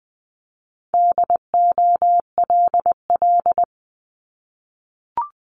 Morse code, heard as a keyed tone, about 20 wpm, 700 Hz. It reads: DOLL
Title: Multi-letter Phonemes 20wpm